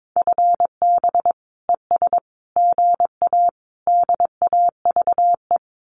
Morse code, heard as a keyed tone, about 22 wpm, 700 Hz. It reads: FB ES GA DAVE
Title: Common QSO Phrases Encore 22wpm